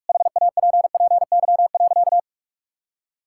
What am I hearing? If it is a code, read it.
Morse code, heard as a keyed tone, about 41 wpm, 700 Hz. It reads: HAPPY.
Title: Sentences from Top 1000 Words - Review Part 6 - 45wpm